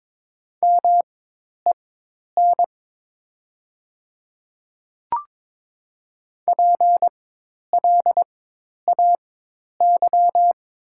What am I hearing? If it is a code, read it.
Morse code, heard as a keyed tone, about 22 wpm, spaced as for 10 wpm, 700 Hz. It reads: MEN PLAY
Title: 4x Character Spacing - Top 200 Words - 22wpm